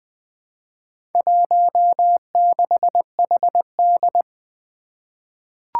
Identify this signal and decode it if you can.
Morse code, heard as a keyed tone, about 20 wpm, 700 Hz. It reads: 16HD
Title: Four Challenging Letter-Number Combinations 20wpm